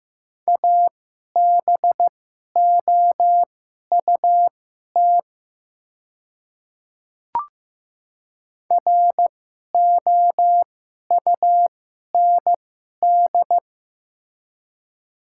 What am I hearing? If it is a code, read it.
Morse code, heard as a keyed tone, about 15 wpm, spaced as for 11 wpm, 700 Hz. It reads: ABOUT ROUND
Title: 2x Character Spacing - Top 200 Words - 15wpm